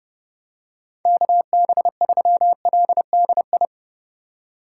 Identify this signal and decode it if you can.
Morse code, heard as a keyed tone, about 32 wpm, 700 Hz. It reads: KB3LDI
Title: USA Call Signs - 8x Character Spacing - 30wpm